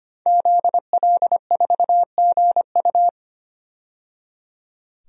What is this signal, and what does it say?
Morse code, heard as a keyed tone, about 25 wpm, 700 Hz. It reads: ZL4GU
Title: DXCC-100 Call Signs - 5 Characters - 25wpm